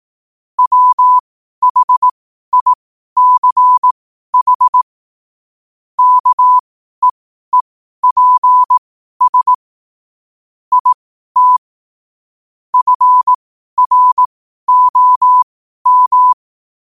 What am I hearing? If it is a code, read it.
Morse code, heard as a keyed tone, about 18 wpm, spaced as for 12 wpm, 1000 Hz. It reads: WHICH KEEPS IT FROM
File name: qotdfiles / 12wpm